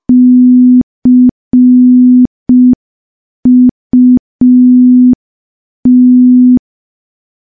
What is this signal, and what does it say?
Morse code, heard as a keyed tone, about 5 wpm, 250 Hz. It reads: CUT
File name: morse / morse-training-05WPM